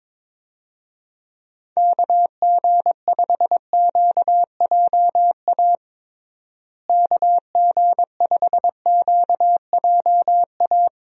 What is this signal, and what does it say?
Morse code, heard as a keyed tone, about 22 wpm, 700 Hz. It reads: KG5QJA KG5QJA
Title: USA Call Signs - 6 Characters - Repeated 3x - 22wpm